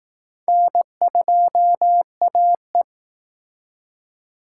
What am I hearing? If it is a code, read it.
Morse code, heard as a keyed tone, about 20 wpm, 700 Hz. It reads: N2AE